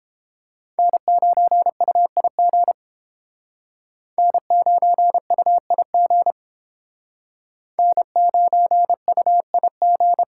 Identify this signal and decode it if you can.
Morse code, heard as a keyed tone, about 30 wpm, 700 Hz. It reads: N9UIG N9UIG N9UIG